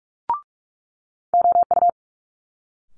Morse code, heard as a keyed tone, about 46 wpm, 700 Hz. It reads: OU
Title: Multi-letter Phonemes Quick Fire 45wpm